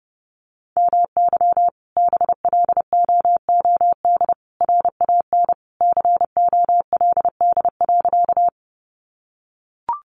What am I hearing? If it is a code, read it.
Morse code, heard as a keyed tone, about 30 wpm, 700 Hz. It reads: MY BLOOD RAN COLD.
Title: Sentences from Top 1100 Words 30wpm